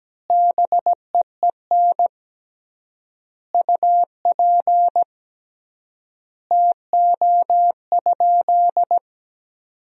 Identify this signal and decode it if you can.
Morse code, heard as a keyed tone, about 17 wpm, 700 Hz. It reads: BEEN UP TO?
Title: 3x Word-Spacing - Sentences from Top 100 Words - 17wpm